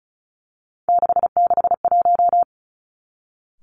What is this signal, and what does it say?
Morse code, heard as a keyed tone, about 36 wpm, 700 Hz. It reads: 661
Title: Three Challenging Letter-Number Combinations 35wpm